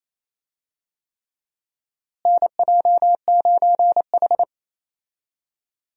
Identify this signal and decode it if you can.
Morse code, heard as a keyed tone, about 28 wpm, 700 Hz. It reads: NJ9H
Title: USA Call Signs - 4 Characters - Repeated 3x - 28wpm